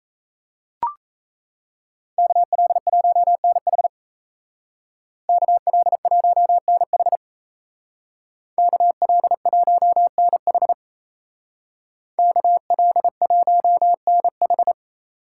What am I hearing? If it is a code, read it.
Morse code, heard as a keyed tone, about 33 wpm, 700 Hz. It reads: KL1NH KL1NH KL1NH KL1NH
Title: Speed Racing - USA 2x2 Call Signs 28wpm